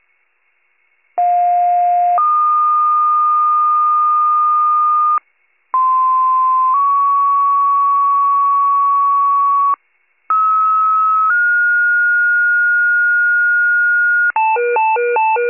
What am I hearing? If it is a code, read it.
Two-tone page, heard as a one-second tone, then a three-second tone, about 710 then 1220 Hz, about 1040 then 1120 Hz, about 1290 then 1430 Hz. It is heard from 1.2 to 14.3 s.